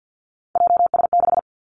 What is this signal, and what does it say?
Morse code, heard as a keyed tone, about 50 wpm, 700 Hz. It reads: JS6
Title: Three Challenging Letter-Number Combinations 50wpm